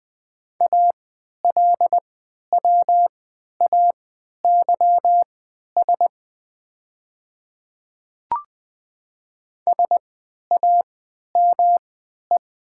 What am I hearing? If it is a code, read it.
Morse code, heard as a keyed tone, about 20 wpm, spaced as for 11 wpm, 700 Hz. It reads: ALWAYS SAME
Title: 3x Character Spacing - Top 500 Words - 20wpm